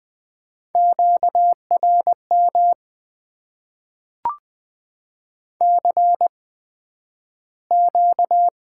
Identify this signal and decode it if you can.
Morse code, heard as a keyed tone, about 20 wpm, spaced as for 11 wpm, 700 Hz. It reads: QRM CQ
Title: Top 90 QSO Elements - 8x Character Spacing - 20wpm